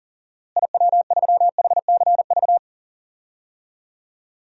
Morse code, heard as a keyed tone, about 37 wpm, 700 Hz. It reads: IW3HCV
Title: DXCC-100 Call Signs - 6 Characters - Repeated 2x - 40wpm